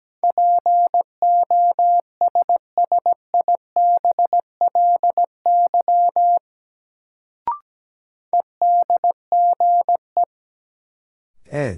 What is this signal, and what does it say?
Morse code, heard as a keyed tone, about 17 wpm, 700 Hz. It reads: POSSIBLY EDGE
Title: Sentences from Top 1300 Words - New Word Drill - 17wpm